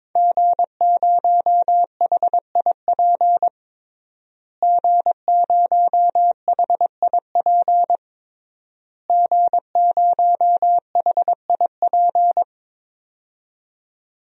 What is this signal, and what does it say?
Morse code, heard as a keyed tone, about 22 wpm, 700 Hz. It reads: G0HIP G0HIP G0HIP
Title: DXCC-100 Call Signs - 5 Characters - Repeated 3x - 22wpm